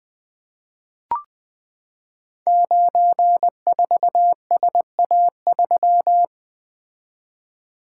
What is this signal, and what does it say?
Morse code, heard as a keyed tone, about 20 wpm, 700 Hz. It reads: 94SA3
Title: Code Groups - 5 Letter:Numbers - 20wpm